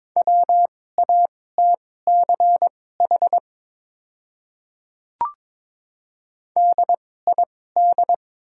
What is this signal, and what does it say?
Morse code, heard as a keyed tone, about 21 wpm, spaced as for 16 wpm, 700 Hz. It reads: WATCH DID
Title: 2x Character Spacing - Top 300 Words - 22wpm